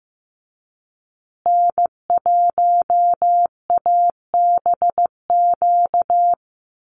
Morse code, heard as a keyed tone, about 15 wpm, 700 Hz. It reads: N1ABQ
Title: USA Call Signs Repeated 3x 15wpm